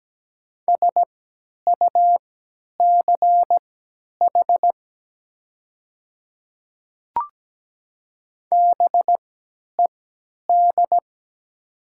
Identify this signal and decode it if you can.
Morse code, heard as a keyed tone, about 17 wpm, spaced as for 10 wpm, 700 Hz. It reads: SUCH BED
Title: 3x Character Spacing - Top 500 Words - 17wpm